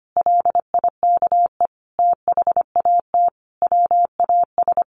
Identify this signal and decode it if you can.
Morse code, heard as a keyed tone, about 25 wpm, 700 Hz. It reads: LIKE THAT WAS
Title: Sentences from Top 1500 Words 25wpm